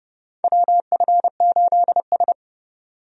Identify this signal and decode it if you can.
Morse code, heard as a keyed tone, about 30 wpm, 700 Hz. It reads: WF8S
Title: USA Call Signs - 4 Characters - 30wpm